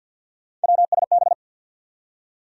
Morse code, heard as a keyed tone, about 50 wpm, 700 Hz. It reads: WSB
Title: USA Call Sign Postfixes - 3 Letters 50wpm